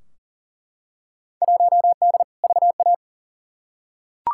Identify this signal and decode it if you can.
Morse code, heard as a keyed tone, about 40 wpm, 700 Hz. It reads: 1D VA
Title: Field Day Exchanges - 40wpm